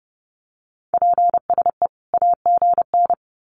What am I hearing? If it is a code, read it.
Morse code, heard as a keyed tone, about 30 wpm, 700 Hz. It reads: PSE AGN
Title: Common QSO Phrases 30wpm